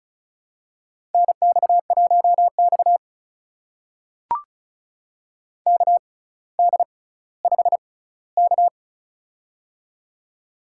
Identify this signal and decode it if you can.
Morse code, heard as a keyed tone, about 36 wpm, spaced as for 18 wpm, 700 Hz. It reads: NX1X KD5K
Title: USA Call Signs - 4 Characters - 6x Character Spacing - 35wpm